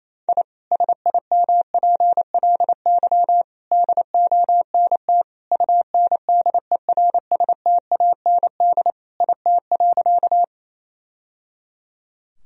Morse code, heard as a keyed tone, about 28 wpm, 700 Hz. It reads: I SIMPLY DONT UNDERSTAND IT.
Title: Sentences from Top 1000 Words - Review Part 4 - 28wpm